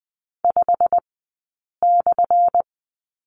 Morse code, heard as a keyed tone, about 20 wpm, spaced as for 8 wpm, 700 Hz. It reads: 5/
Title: 197 Lesson 32d - Letter k - Two Characters 20wpm